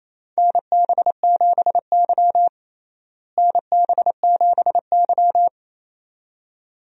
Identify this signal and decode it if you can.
Morse code, heard as a keyed tone, about 27 wpm, 700 Hz. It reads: NB7Y NB7Y
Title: USA Call Signs - 4 Characters - Repeated 2x - 28wpm